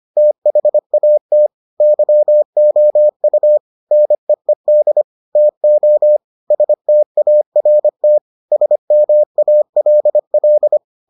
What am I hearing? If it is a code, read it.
Morse code, heard as a keyed tone, about 25 wpm, 600 Hz. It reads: THAT YOU NEED TO START SMALL